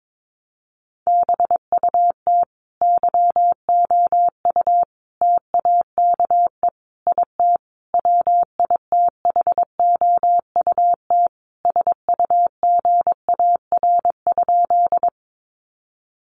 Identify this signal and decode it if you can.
Morse code, heard as a keyed tone, about 22 wpm, 700 Hz. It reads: BUT YOU TAKE IT WITHOUT SUGAR?
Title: Sentences from Top 1200 Words 22wpm